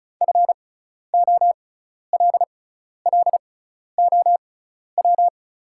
Morse code, heard as a keyed tone, about 36 wpm, spaced as for 12 wpm, 700 Hz. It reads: FOLLOW
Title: 6x Character Spacing - Top 500 Words - 35wpm